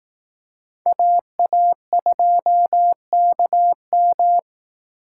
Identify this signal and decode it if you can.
Morse code, heard as a keyed tone, about 18 wpm, 700 Hz. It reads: AA2KM
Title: Speed Racing - USA Call Signs 15wpm